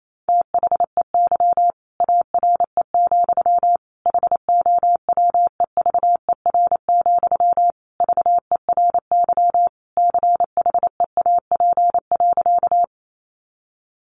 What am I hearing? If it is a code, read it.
Morse code, heard as a keyed tone, about 28 wpm, 700 Hz. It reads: THEY ARE, HOWEVER, VERY CHEAP.
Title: Sentences from Top 1200 Words 28wpm